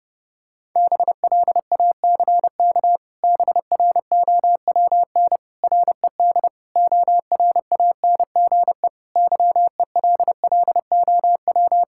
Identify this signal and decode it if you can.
Morse code, heard as a keyed tone, about 30 wpm, 700 Hz. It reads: BLACK BROWN RED ORANGE YELLOW
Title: Pangram Challenge v2 - 30wpm